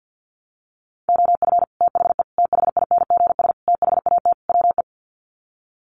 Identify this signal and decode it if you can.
Morse code, heard as a keyed tone, about 50 wpm, 700 Hz. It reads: OF THE THINGS THAT WE
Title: Sets of 5 Words Encore 50wpm